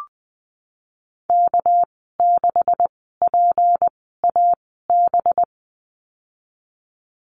Morse code, heard as a keyed tone, about 20 wpm, spaced as for 14 wpm, 700 Hz. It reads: K6PAB
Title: 213 Lesson 34g - Number 8 - Call Signs 20wpm